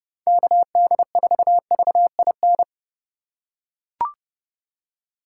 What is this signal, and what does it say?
Morse code, heard as a keyed tone, about 32 wpm, 700 Hz. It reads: KD4VIN